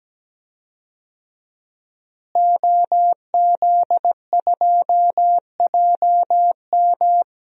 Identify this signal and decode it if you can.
Morse code, heard as a keyed tone, about 17 wpm, 700 Hz. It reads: OZ2JM
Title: DXCC-100 Call Signs - 5 Characters - Repeated 3x - 17wpm